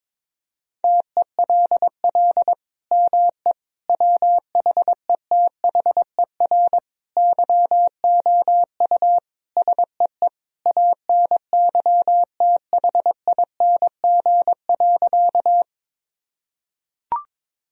Morse code, heard as a keyed tone, about 22 wpm, 700 Hz. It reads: TELL ME WHETHER YOU SEE ANYTHING.
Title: Sentences from Top 1000 Words - Review Part 3 - 22wpm